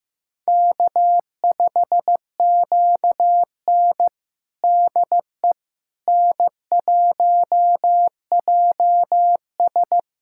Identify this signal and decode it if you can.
Morse code, heard as a keyed tone, about 15 wpm, 700 Hz. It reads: K5QN DE N1JS
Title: USA Callsign DE Callsign - 4 Characters - 15wpm